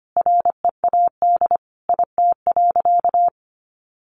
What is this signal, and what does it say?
Morse code, heard as a keyed tone, about 25 wpm, 700 Hz. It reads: READ IT.
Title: Sentences from Top 1300 Words 25wpm